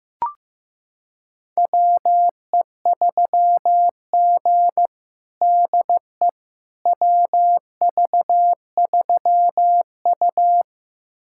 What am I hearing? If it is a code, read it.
Morse code, heard as a keyed tone, about 16 wpm, 700 Hz. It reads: WE3G DE WV3U
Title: USA Callsign DE Callsign - 4 Characters - 15wpm